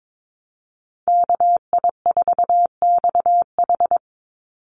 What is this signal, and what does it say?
Morse code, heard as a keyed tone, about 22 wpm, 700 Hz. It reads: KI4XH
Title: Call Signs Repeated 3x - 22wpm